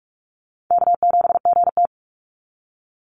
Morse code, heard as a keyed tone, about 46 wpm, 690 Hz. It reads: K7GT
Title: NA SOTA Chaser Callsigns 45wpm